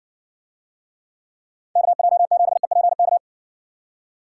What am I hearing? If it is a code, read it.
Morse code, heard as a keyed tone, about 60 wpm, 700 Hz. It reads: KJ7EPK